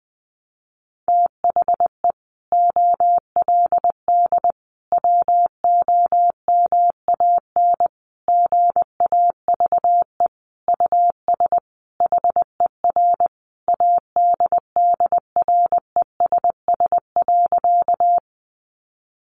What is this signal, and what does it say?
Morse code, heard as a keyed tone, about 19 wpm, 700 Hz. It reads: THE OLD WOMAN GAVE US HER ADDRESS.
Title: Sentences from Top 1100 Words 20wpm